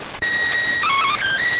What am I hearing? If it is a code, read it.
SSTV (slow-scan television), PD 290.